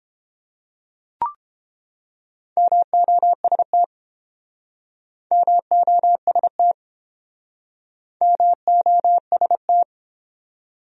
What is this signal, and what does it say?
Morse code, heard as a keyed tone, about 31 wpm, 700 Hz. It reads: MOST MOST MOST